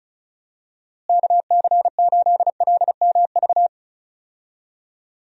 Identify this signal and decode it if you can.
Morse code, heard as a keyed tone, about 36 wpm, 700 Hz. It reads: KC8LMV
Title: USA Call Signs - 6 Characters - Repeated 2x - 35wpm